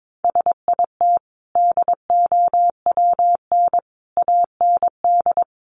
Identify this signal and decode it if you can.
Morse code, heard as a keyed tone, about 22 wpm, 700 Hz. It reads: SIT DOWN AND
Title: Sentences from Top 500 Words 22wpm